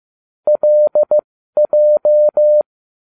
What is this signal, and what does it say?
Morse code, heard as a keyed tone, about 15 wpm, spaced as for 12 wpm, 600 Hz. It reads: LJ